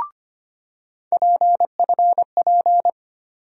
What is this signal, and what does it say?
Morse code, heard as a keyed tone, about 25 wpm, 700 Hz. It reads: PFP
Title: USA Call Sign Postfixes - 3 Letters 25wpm